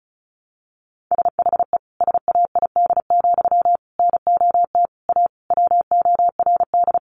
Morse code, heard as a keyed tone, about 34 wpm, 700 Hz. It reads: SHE SAID, NOT A WORD
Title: Sentences from Top 800 Words 35wpm